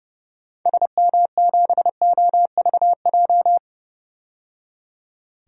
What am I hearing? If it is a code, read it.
Morse code, heard as a keyed tone, about 31 wpm, 690 Hz. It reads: SM7OVJ
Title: DXCC-100 Call Signs - 6 Characters - 30wpm